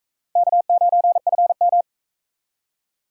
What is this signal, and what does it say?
Morse code, heard as a keyed tone, about 40 wpm, 700 Hz. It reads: K9FM